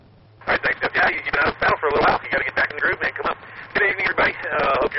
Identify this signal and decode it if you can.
Emergency Alert System (SAME) tones, heard as harsh buzzing bursts, with speech. It is heard from 0.6 to 4.4 s.